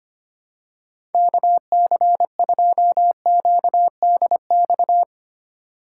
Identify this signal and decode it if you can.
Morse code, heard as a keyed tone, about 25 wpm, 700 Hz. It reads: KC2QDX
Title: USA Call Signs - 6 Characters - 2x Character Spacing - 25wpm